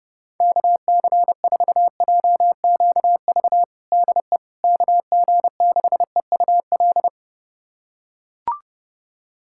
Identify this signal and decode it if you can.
Morse code, heard as a keyed tone, about 32 wpm, 700 Hz. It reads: KC4JQV DE KG6EUL
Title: USA Callsign DE Callsign - 6 Characters - 30wpm